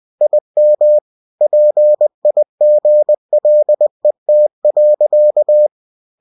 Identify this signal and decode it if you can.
Morse code, heard as a keyed tone, about 20 wpm, 600 Hz. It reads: IM PIGLET.